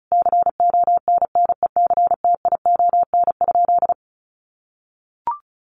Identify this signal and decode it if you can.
Morse code, heard as a keyed tone, about 34 wpm, 700 Hz. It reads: CONNECTION?
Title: Sentences from Top 1300 Words 35wpm